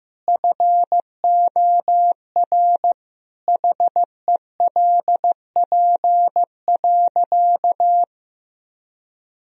Morse code, heard as a keyed tone, about 15 wpm, 700 Hz. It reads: FOR HELP.
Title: Sentences from Top 1000 Words - Review Part 2 - 15wpm